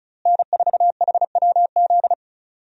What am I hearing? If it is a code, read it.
Morse code, heard as a keyed tone, about 34 wpm, 700 Hz. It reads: N4HWZ
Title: USA Call Signs - 5 Characters - Repeated 2x - 35wpm